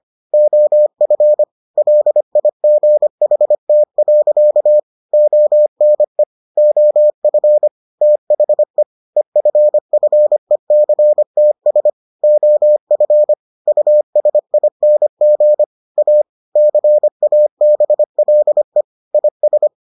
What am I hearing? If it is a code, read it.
Morse code, heard as a keyed tone, about 25 wpm, 600 Hz. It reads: OF LIGHT. ONE OF THE EFFECTS OF USING A CABLE IS